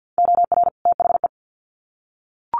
Morse code, heard as a keyed tone, about 50 wpm, 700 Hz. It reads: OR THE
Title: Sets of 2 Words 50wpm